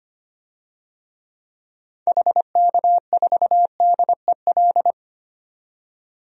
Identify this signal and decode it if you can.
Morse code, heard as a keyed tone, about 25 wpm, 700 Hz. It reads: HK4DEL
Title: DXCC-100 Call Signs - 6 Characters - 25wpm